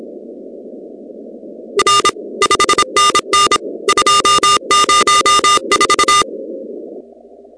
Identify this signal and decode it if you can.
Morse code, heard as a keyed tone, about 27 wpm, 1210 Hz, with noise. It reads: R 5NN 204
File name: contests / 2019-cw-wpx-kt5j-k5pi- -2330 0-16